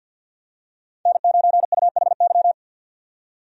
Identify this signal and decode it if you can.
Morse code, heard as a keyed tone, about 53 wpm, 700 Hz. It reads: N9UHY
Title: USA Call Signs - 5 Characters - 50wpm